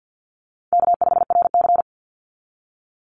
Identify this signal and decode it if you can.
Morse code, heard as a keyed tone, about 50 wpm, 700 Hz. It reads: K5RC
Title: Easy Call Signs Repeated 2x - 50wpm